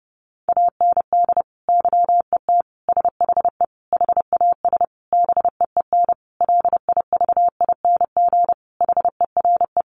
Morse code, heard as a keyed tone, about 28 wpm, 700 Hz. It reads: AND YET SHE HAS BEEN LIVING HERE
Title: Sentences from Top 1100 Words 30wpm